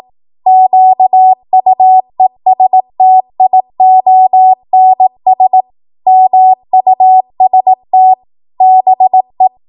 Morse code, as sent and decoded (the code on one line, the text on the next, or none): --.- ..- . ... - .. --- -. ... / -- ..- ... - / -... .
QUESTIONS MUST BE